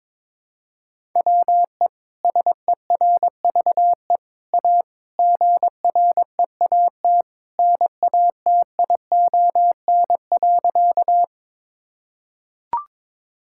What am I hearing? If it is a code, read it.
Morse code, heard as a keyed tone, about 22 wpm, 700 Hz. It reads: WE SERVE A GREAT NATION.
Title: Sentences from Top 1000 Words - Review Part 6 - 22wpm